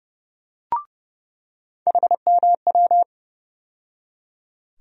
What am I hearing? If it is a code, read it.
Morse code, heard as a keyed tone, about 30 wpm, 700 Hz. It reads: HMW
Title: USA Call Sign Postfixes - 3 Letters 30wpm